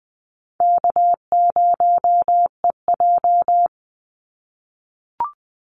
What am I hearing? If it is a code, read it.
Morse code, heard as a keyed tone, about 20 wpm, 700 Hz. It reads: K0EJ
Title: Call Signs - 20wpm